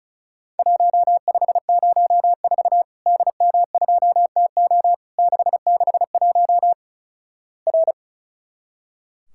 Morse code, heard as a keyed tone, about 36 wpm, 700 Hz. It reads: 1504 DM2TO 661 R
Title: Worked All Europe DX Contest - QTC Exchange - 35wpm